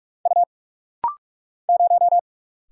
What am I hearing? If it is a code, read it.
Morse code, heard as a keyed tone, about 44 wpm, 700 Hz. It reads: U 0
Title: Single Letter-Number Alt - Rapid-Fire 45wpm